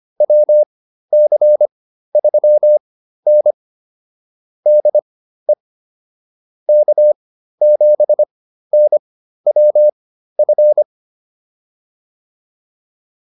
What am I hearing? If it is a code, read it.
Morse code, heard as a keyed tone, about 25 wpm, spaced as for 13 wpm, 600 Hz. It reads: WC3N DE K7NWF